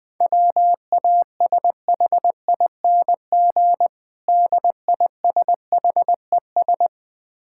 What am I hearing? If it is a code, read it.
Morse code, heard as a keyed tone, about 20 wpm, 700 Hz. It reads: WASHING DISHES
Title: Easy Words - Speed-Racing with Hints - 20wpm